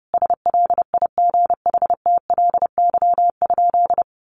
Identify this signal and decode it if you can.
Morse code, heard as a keyed tone, about 28 wpm, 700 Hz. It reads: SLIGHTLY?
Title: Sentences from Top 1300 Words 30wpm